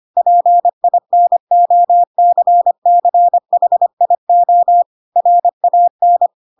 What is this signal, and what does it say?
Morse code, heard as a keyed tone, about 25 wpm, 700 Hz. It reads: PINOCCHIO RAN
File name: Pinocchio / Pinocchio ch3 25 25 ik1hge